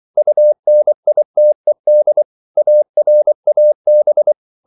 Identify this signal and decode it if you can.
Morse code, heard as a keyed tone, about 24 wpm, 600 Hz. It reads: UNITED ARAB
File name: CW / Practice / NationalCapitals24